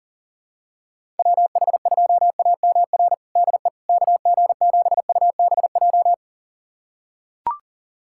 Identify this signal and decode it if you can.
Morse code, heard as a keyed tone, about 40 wpm, 700 Hz. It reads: WH2AMR DE KG7UBJ
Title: USA Callsign DE Callsign - 6 Characters - 40wpm